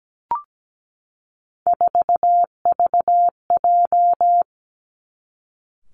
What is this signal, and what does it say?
Morse code, heard as a keyed tone, about 17 wpm, 700 Hz. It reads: 4VJ
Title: Three Challenging Letter-Number Combinations 17wpm